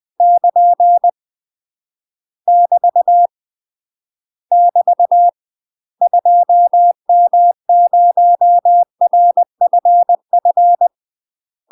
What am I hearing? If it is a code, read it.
Morse code, heard as a keyed tone, about 20 wpm, 700 Hz. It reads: <KN> <BT> <BT> 2M0RFF
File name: UK MASTER QSOs TRAINING / 20WPM / UK MASTER QSOs 0071